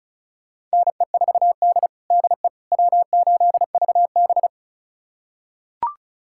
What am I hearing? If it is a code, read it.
Morse code, heard as a keyed tone, about 36 wpm, 700 Hz. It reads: NE4D DE W8VB
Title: USA Callsign DE Callsign - 4 Characters - 35wpm